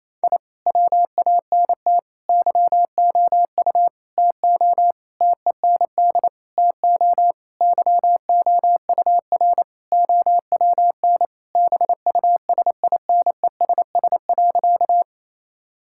Morse code, heard as a keyed tone, about 28 wpm, 700 Hz. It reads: I WANT YOU TO TEND TO YOUR OWN BUSINESS.
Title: Sentences from Top 1000 Words - Review Part 5 - 28wpm